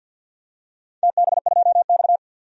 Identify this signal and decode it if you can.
Morse code, heard as a keyed tone, about 47 wpm, 700 Hz. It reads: TB2X
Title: DXCC-100 Call Signs - 4 Characters - 50wpm